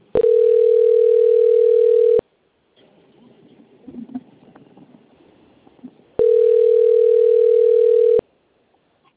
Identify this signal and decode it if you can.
Ringback tone (North America), 440 + 480 Hz.
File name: out- - - - - .56409